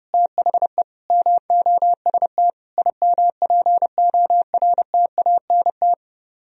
Morse code, heard as a keyed tone, about 35 wpm, 700 Hz. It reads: THE MOST IMPORTANT
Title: Speed-Racing - Sets of 4 Words 30wpm